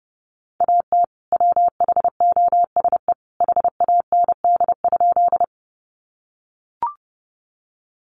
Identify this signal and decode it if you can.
Morse code, heard as a keyed tone, about 30 wpm, 700 Hz. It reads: AT WHOSE HAND?
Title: Sentences from Top 1200 Words 30wpm